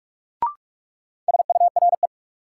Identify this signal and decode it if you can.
Morse code, heard as a keyed tone, about 46 wpm, 700 Hz. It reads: SURE
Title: Sets of 3 Words - New Word Drill - 45wpm